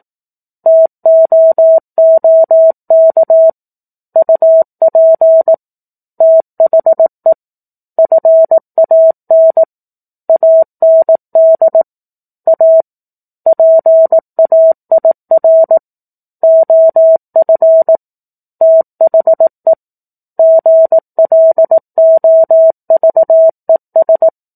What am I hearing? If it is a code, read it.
Morse code, heard as a keyed tone, about 18 wpm, 650 Hz. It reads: TOOK UP THE FAN AND A PAIR OF THE GLOVES
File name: 18 WPM